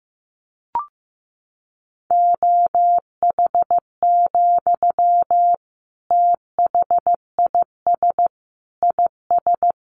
Morse code, heard as a keyed tone, about 15 wpm, 700 Hz. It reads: OH, THIS IS
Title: Sentences from Top 500 Words 15wpm